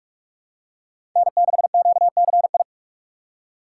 Morse code, heard as a keyed tone, about 42 wpm, 700 Hz. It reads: N6QCI
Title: USA Call Signs - 5 Characters - 3x Character Spacing - 45wpm